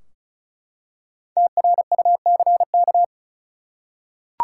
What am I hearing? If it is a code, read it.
Morse code, heard as a keyed tone, about 36 wpm, 700 Hz. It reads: TRUCK
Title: Five Letter Words - 35wpm